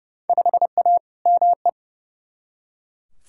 Morse code, heard as a keyed tone, about 30 wpm, 700 Hz. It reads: 5A ME
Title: Field Day Exchanges Encore - 30wpm